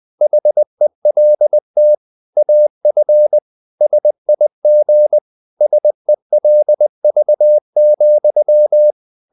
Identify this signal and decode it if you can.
Morse code, heard as a keyed tone, about 20 wpm, 600 Hz. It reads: HELT AF SIG SELV,
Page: Ferie 20 wpm